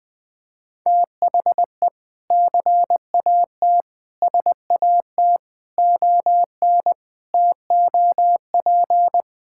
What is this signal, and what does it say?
Morse code, heard as a keyed tone, about 19 wpm, 700 Hz. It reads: THE CAT SAT ON TOP
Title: Sentences with 4-Word Phrases - Spoken Beginning - 20wpm